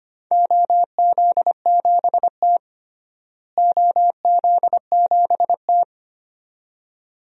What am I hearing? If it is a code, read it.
Morse code, heard as a keyed tone, about 25 wpm, 700 Hz. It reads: OZ7T OZ7T
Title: DXCC-100 Call Signs - 4 Characters - Repeated 2x - 25wpm